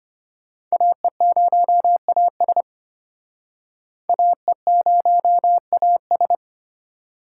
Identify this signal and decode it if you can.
Morse code, heard as a keyed tone, about 34 wpm, 700 Hz. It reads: AE0AS AE0AS